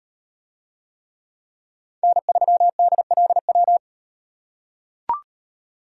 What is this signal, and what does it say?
Morse code, heard as a keyed tone, about 38 wpm, 700 Hz. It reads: N3DLW